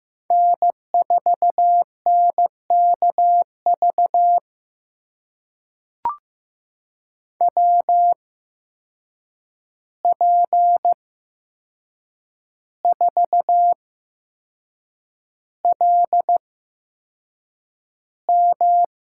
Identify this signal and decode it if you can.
Morse code, heard as a keyed tone, about 15 wpm, spaced as for 6 wpm, 700 Hz. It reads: N4NKV WP4LM
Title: USA Call Signs - 5 Characters - 8x Character Spacing - 15wpm